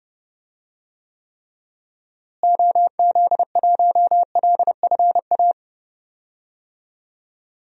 Morse code, heard as a keyed tone, about 30 wpm, 700 Hz. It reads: OZ1LFA
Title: DXCC-100 Call Signs - 6 Characters - Repeated 3x - 30wpm